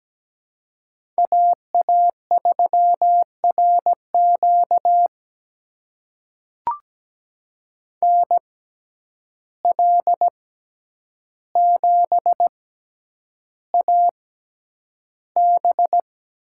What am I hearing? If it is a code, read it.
Morse code, heard as a keyed tone, about 17 wpm, spaced as for 9 wpm, 700 Hz. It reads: AA3RQ NL7AB
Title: USA Call Signs - 6x Character Spacing - 17wpm